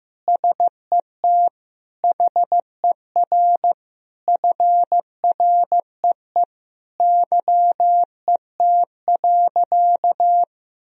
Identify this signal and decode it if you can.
Morse code, heard as a keyed tone, about 15 wpm, 700 Hz. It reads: SET HER FREE YET.
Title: Sentences from Top 1000 Words - Review Part 1 - 15wpm